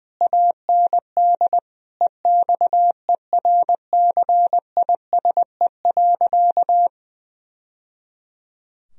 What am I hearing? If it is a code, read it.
Morse code, heard as a keyed tone, about 20 wpm, 700 Hz. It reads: AND EXERCISE.
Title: Sentences from Top 1000 Words - Review Part 4 - 20wpm